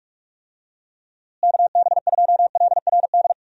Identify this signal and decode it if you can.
Morse code, heard as a keyed tone, about 43 wpm, 700 Hz. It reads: KB2LRD
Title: USA Call Signs - 6 Characters - Repeated 3x - 45wpm